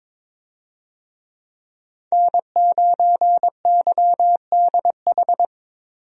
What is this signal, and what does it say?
Morse code, heard as a keyed tone, about 22 wpm, 700 Hz. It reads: N9YDH
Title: USA Call Signs - 5 Characters - Repeated 3x - 22wpm